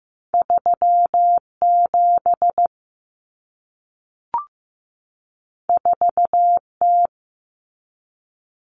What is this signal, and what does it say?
Morse code, heard as a keyed tone, about 15 wpm, 700 Hz. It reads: 37 4T
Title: Common Cut Numbers - 2-Digit 15wpm